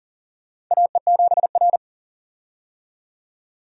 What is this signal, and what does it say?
Morse code, heard as a keyed tone, about 40 wpm, 700 Hz. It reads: AE7R
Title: USA Call Signs - 4 Characters - 40wpm